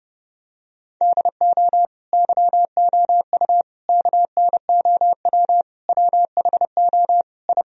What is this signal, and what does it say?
Morse code, heard as a keyed tone, about 30 wpm, 700 Hz. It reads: DO YOU KNOW WHO I
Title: Sentences from Top 1000 Words - Review Part 2 - 30wpm